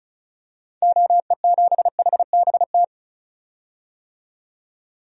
Morse code, heard as a keyed tone, about 36 wpm, 700 Hz. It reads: OE7HBT
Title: DXCC-100 Call Signs - 6 Characters - 35wpm